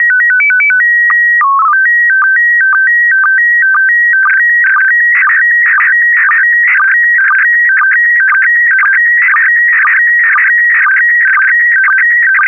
SSTV (slow-scan television), PD 120.
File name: ISS-SSTV / SSTV-Audio-ISS-Bild-04